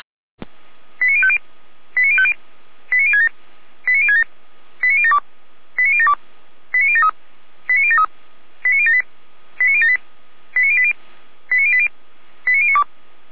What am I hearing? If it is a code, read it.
Five-tone selective call (ZVEI), heard as quick runs of five tones, tones of about 70 ms. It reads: 89050 89050 89067 89067 89072 89072 89073 89073 89078 89078 89080 89080 89092
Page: SDR/ZVEI